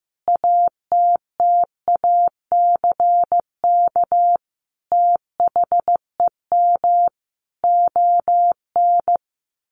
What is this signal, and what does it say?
Morse code, heard as a keyed tone, about 15 wpm, 700 Hz. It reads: ATTACK THEM ON
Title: Sentences from Top 1500 Words 15wpm